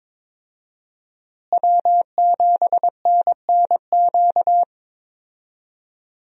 Morse code, heard as a keyed tone, about 22 wpm, 700 Hz. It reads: W7NNQ
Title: USA Call Signs - 5 Characters - Repeated 2x - 22wpm